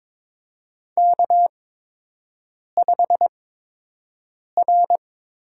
Morse code, heard as a keyed tone, about 22 wpm, spaced as for 6 wpm, 700 Hz. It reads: K5R
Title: USA Call Signs - 8x Character Spacing - 22wpm